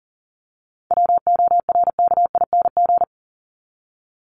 Morse code, heard as a keyed tone, about 40 wpm, 700 Hz. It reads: WORKING
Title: Top 501-750 Words 40wpm